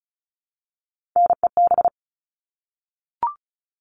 Morse code, heard as a keyed tone, about 34 wpm, 700 Hz. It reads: NEB
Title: All Cut Numbers - 3-Digit 35wpm